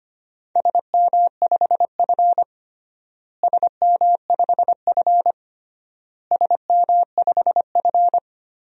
Morse code, heard as a keyed tone, about 25 wpm, 700 Hz. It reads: SM5F SM5F SM5F
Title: DXCC-100 Call Signs - 4 Characters - Repeated 3x - 25wpm